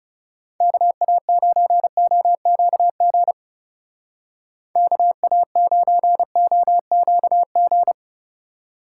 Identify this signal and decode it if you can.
Morse code, heard as a keyed tone, about 36 wpm, 700 Hz. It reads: KA9OQG KA9OQG
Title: Speed Racing - USA 2x3 Call Signs 30wpm